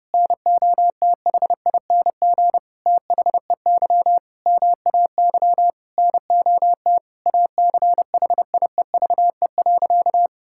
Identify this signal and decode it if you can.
Morse code, heard as a keyed tone, about 31 wpm, 700 Hz. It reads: NOTHING THEY MAY NOT ACHIEVE.
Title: Sentences from Top 1000 Words - Review Part 6 - 30wpm